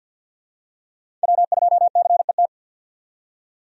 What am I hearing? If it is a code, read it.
Morse code, heard as a keyed tone, about 52 wpm, 700 Hz. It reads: W2CET
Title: USA Call Signs - 5 Characters - 50wpm